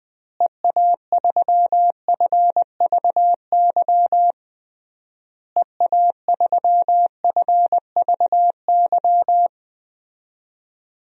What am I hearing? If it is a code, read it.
Morse code, heard as a keyed tone, about 20 wpm, 700 Hz. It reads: EA3FVY EA3FVY
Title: DXCC-100 Call Signs - 6 Characters - Repeated 3x - 20wpm